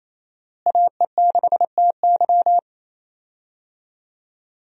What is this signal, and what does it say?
Morse code, heard as a keyed tone, about 28 wpm, 700 Hz. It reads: AE6TY
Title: USA Call Signs - 5 Characters - Repeated 2x - 28wpm